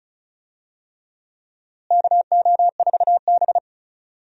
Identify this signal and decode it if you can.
Morse code, heard as a keyed tone, about 34 wpm, 700 Hz. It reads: KO4B
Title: USA Call Signs - 4 Characters - Repeated 3x - 35wpm